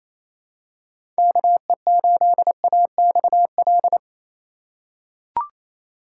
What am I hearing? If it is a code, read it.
Morse code, heard as a keyed tone, about 27 wpm, 700 Hz. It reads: KE8AXL
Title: USA Call Signs - 6 Characters - 2x Character Spacing - 28wpm